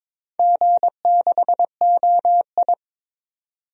Morse code, heard as a keyed tone, about 22 wpm, 700 Hz. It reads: G6OI
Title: DXCC-100 Call Signs - 4 Characters - 22wpm